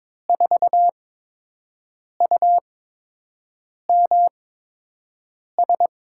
Morse code, heard as a keyed tone, about 22 wpm, spaced as for 6 wpm, 700 Hz. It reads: 4UMS
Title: USA Call Signs - 8x Character Spacing - 22wpm